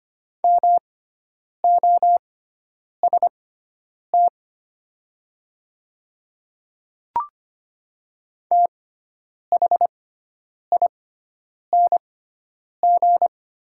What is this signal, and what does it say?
Morse code, heard as a keyed tone, about 25 wpm, spaced as for 9 wpm, 700 Hz. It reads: MOST THING
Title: 6x Character Spacing - Top 100 Words - 25wpm